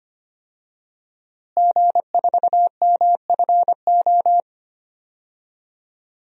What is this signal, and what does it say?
Morse code, heard as a keyed tone, about 25 wpm, 700 Hz. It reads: G4MFO
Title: DXCC-100 Call Signs - 5 Characters - Repeated 2x - 25wpm